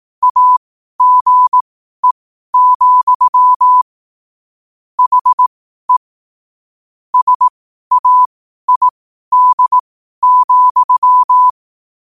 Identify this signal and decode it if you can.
Morse code, heard as a keyed tone, about 18 wpm, spaced as for 12 wpm, 1000 Hz. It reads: AGE, HE SAID,
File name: qotdfiles / 12wpm